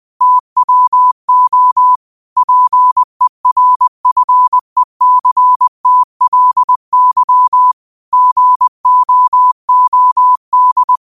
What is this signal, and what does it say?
Morse code, heard as a keyed tone, about 20 wpm, 1000 Hz. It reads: TWO PERFECTLY GOOD
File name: qotdfiles / 20wpm